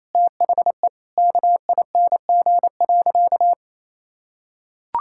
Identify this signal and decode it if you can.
Morse code, heard as a keyed tone, about 28 wpm, 700 Hz. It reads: THE KING.
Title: Sentences from Top 1000 Words - Review Part 6 - 28wpm